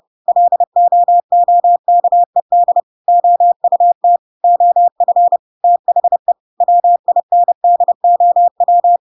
Morse code, heard as a keyed tone, about 30 wpm, 700 Hz. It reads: LOOKED OUT OF THE WINDOW
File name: Pinocchio / Pinocchio ch7 30 30 ik1hge